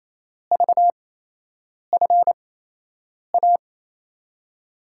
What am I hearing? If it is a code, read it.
Morse code, heard as a keyed tone, about 28 wpm, spaced as for 8 wpm, 700 Hz. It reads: VFA
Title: USA Call Signs - 6 Characters - 8x Character Spacing - 28wpm